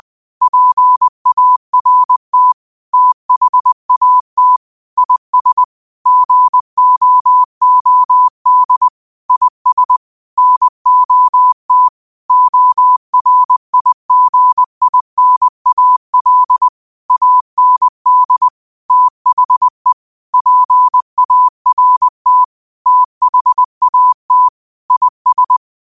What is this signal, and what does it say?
Morse code, heard as a keyed tone, about 20 wpm, 1000 Hz. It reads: PART THAT IS GOOD IS NOT ORIGINAL AND THE PART THAT IS